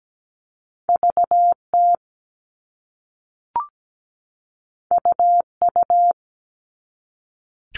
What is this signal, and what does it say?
Morse code, heard as a keyed tone, about 17 wpm, 700 Hz. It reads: VT UU
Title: All Cut Numbers - 2-Digit 17wpm